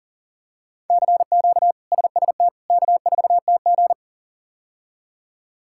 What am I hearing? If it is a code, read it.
Morse code, heard as a keyed tone, about 38 wpm, 700 Hz. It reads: CQ SST K4TG
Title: K1USN SST QSO Practice - 40wpm